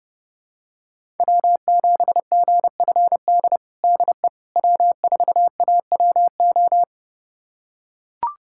Morse code, heard as a keyed tone, about 30 wpm, 700 Hz. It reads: W7GFD DE W4AWO
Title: USA Callsign DE Callsign - 5 Characters - 30wpm